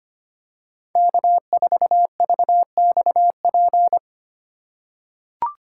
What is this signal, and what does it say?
Morse code, heard as a keyed tone, about 25 wpm, 700 Hz. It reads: K4VXP
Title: USA Call Signs - 5 Characters - 2x Character Spacing - 25wpm